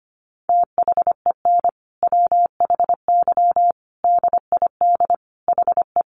Morse code, heard as a keyed tone, about 25 wpm, 700 Hz. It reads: THEN WHY DID HE
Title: Sentences from Top 800 Words 25wpm